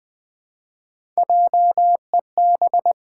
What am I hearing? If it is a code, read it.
Morse code, heard as a keyed tone, about 21 wpm, 700 Hz. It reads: JEB
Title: Pangram Challenge v2 - 20wpm